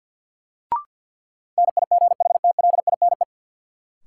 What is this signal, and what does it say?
Morse code, heard as a keyed tone, about 50 wpm, 700 Hz. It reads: NIGHTLINE
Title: Sets of 5 Words - New Word Drill - 50wpm